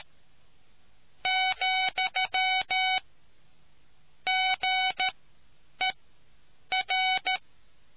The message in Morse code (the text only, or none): , GER